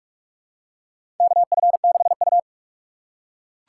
Morse code, heard as a keyed tone, about 46 wpm, 700 Hz. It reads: KF6U
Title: USA Call Signs - 4 Characters - 4x Character Spacing - 45wpm